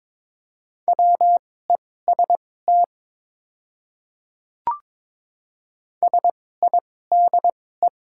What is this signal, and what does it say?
Morse code, heard as a keyed tone, about 22 wpm, spaced as for 16 wpm, 700 Hz. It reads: WEST SIDE
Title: 2x Character Spacing - Top 500 Words - 22wpm